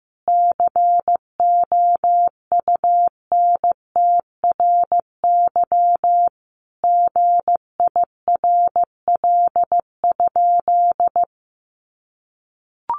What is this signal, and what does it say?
Morse code, heard as a keyed tone, about 15 wpm, 700 Hz. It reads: COUNTRY GIRL?
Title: Sentences from Top 1500 Words 15wpm